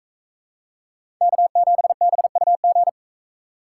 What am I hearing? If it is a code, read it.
Morse code, heard as a keyed tone, about 44 wpm, 700 Hz. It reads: K7BUG